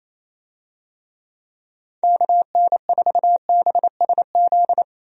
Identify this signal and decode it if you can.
Morse code, heard as a keyed tone, about 27 wpm, 700 Hz. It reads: KN4BSZ
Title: USA Call Signs - 6 Characters - 28wpm